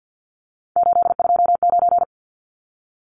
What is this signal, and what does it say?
Morse code, heard as a keyed tone, about 50 wpm, 700 Hz. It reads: 829
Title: Numbers - 3-Digit 50wpm